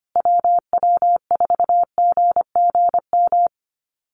Morse code, heard as a keyed tone, about 25 wpm, 700 Hz. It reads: WW4GGM
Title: USA Call Signs Repeated 2x 25wpm